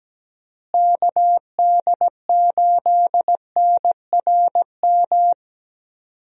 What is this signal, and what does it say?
Morse code, heard as a keyed tone, about 17 wpm, 700 Hz. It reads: KD8NRM
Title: USA Call Signs - 6 Characters - Repeated 2x - 17wpm